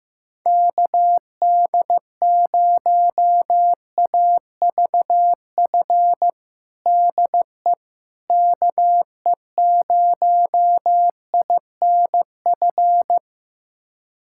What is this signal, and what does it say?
Morse code, heard as a keyed tone, about 15 wpm, 700 Hz. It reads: KD0AVF DE KE0INF
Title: USA Callsign DE Callsign 15wpm